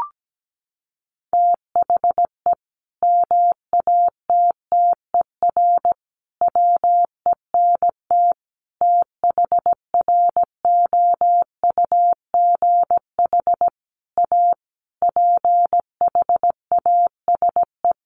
Morse code, heard as a keyed tone, about 17 wpm, 700 Hz. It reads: THE MATTER WENT THROUGH A PHASE
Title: Sentences from Top 1200 Words 17wpm